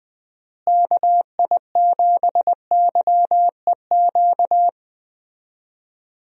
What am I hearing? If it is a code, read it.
Morse code, heard as a keyed tone, about 21 wpm, 700 Hz. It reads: KI7YEQ
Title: USA Call Signs - 6 Characters - Repeated 3x - 20wpm